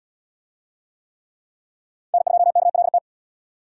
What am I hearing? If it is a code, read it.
Morse code, heard as a keyed tone, about 78 wpm, 700 Hz. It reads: N2KLT